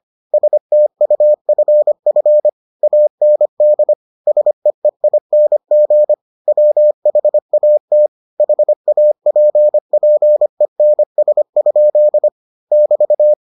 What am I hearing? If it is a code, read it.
Morse code, heard as a keyed tone, about 25 wpm, 600 Hz. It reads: STUFF AND SEEING WHAT HAPPENS? <BT>